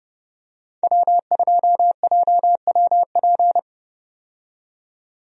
Morse code, heard as a keyed tone, about 30 wpm, 700 Hz. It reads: W2JWP
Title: USA Call Signs - 5 Characters - 30wpm